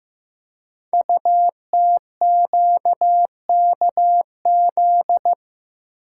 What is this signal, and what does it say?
Morse code, heard as a keyed tone, about 15 wpm, 700 Hz. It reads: UTQKZ
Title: Code Groups - 5 Letters - 15wpm